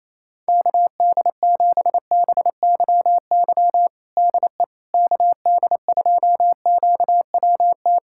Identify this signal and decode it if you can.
Morse code, heard as a keyed tone, about 28 wpm, 700 Hz. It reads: KD7BYY DE KD2QWT
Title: USA Callsign DE Callsign - 6 Characters - 28wpm